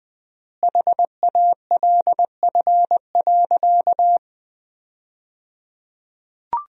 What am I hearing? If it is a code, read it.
Morse code, heard as a keyed tone, about 20 wpm, 700 Hz. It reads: HALF.
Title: 3x Word-Spacing - Sentences from Top 400 Words - 20wpm